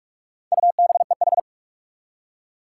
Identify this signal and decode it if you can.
Morse code, heard as a keyed tone, about 43 wpm, 700 Hz. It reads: UBEH
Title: Four Challenging Letter-Number Combinations 45wpm